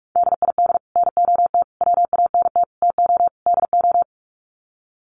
Morse code, heard as a keyed tone, about 44 wpm, 700 Hz. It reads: DID NOT WANT TO DO
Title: Sets of 5 Words Encore 45wpm